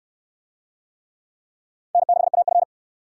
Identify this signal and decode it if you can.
Morse code, heard as a keyed tone, about 68 wpm, 700 Hz. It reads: N6AF